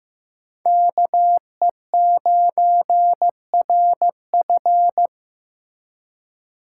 Morse code, heard as a keyed tone, about 15 wpm, 700 Hz. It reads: KE9RF
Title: USA Call Signs - 5 Characters - 3x Character Spacing - 15wpm